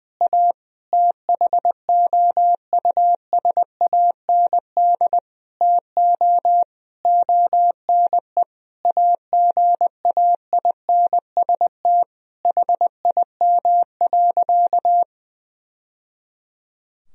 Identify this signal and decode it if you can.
Morse code, heard as a keyed tone, about 20 wpm, 700 Hz. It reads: A THOUSAND TO ONE AGAINST HIM.
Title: Sentences from Top 1000 Words - Review Part 3 - 20wpm